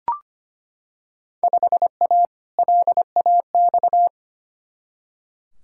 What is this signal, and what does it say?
Morse code, heard as a keyed tone, about 25 wpm, 700 Hz. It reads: 5A LAX
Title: Field Day Exchanges - 25wpm